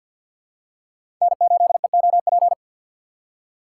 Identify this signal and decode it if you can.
Morse code, heard as a keyed tone, about 52 wpm, 700 Hz. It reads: N8EOP
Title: USA Call Signs - 3x Character Spacing - 50wpm